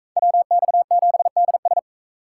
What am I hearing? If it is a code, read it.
Morse code, heard as a keyed tone, about 40 wpm, 700 Hz. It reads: WX7DS